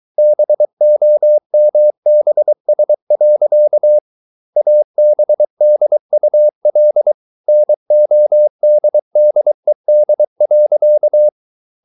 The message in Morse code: BOMBS. ABDUL NODDED.

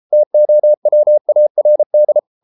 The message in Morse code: TOWARD

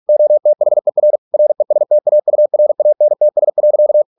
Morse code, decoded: OTHER RESTAURANTS.

